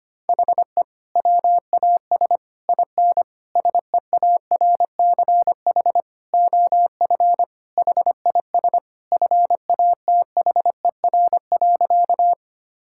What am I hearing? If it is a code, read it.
Morse code, heard as a keyed tone, about 25 wpm, 700 Hz. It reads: HE WAS IN SEARCH OF HIS FATHER.
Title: Sentences from Top 1000 Words - Review Part 5 - 25wpm